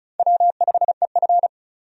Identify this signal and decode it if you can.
Morse code, heard as a keyed tone, about 36 wpm, 700 Hz. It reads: W5EF
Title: USA Call Signs - 4 Characters - 35wpm